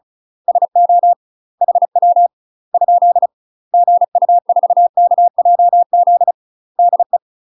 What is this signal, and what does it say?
Morse code, heard as a keyed tone, about 34 wpm, 700 Hz. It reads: SO HW ? GU4KJZ DE
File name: UK MASTER QSOs TRAINING / 35WPM / UK MASTER QSOs 0021